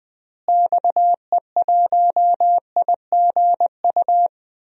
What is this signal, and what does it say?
Morse code, heard as a keyed tone, about 20 wpm, 700 Hz. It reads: XE1IGU
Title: DXCC-100 Call Signs - 6 Characters - Repeated 3x - 20wpm